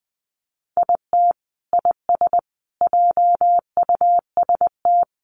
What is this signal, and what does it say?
Morse code, heard as a keyed tone, about 19 wpm, 700 Hz. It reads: IT IS JUST
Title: Sentences from Top 600 Words 20wpm